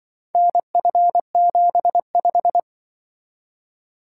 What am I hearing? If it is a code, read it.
Morse code, heard as a keyed tone, about 23 wpm, 700 Hz. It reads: NF75
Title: Speed-Racing - Four Letter-Number Combinations - 20wpm